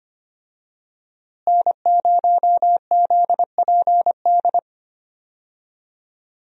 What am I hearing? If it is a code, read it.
Morse code, heard as a keyed tone, about 25 wpm, 700 Hz. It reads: N0ZPD
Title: USA Call Signs - 5 Characters - Repeated 3x - 25wpm